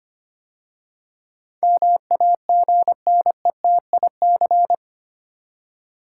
Morse code, heard as a keyed tone, about 25 wpm, 690 Hz. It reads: MAGNETIC
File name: 8 Letter Words - First 3 Letters Given - 25wpm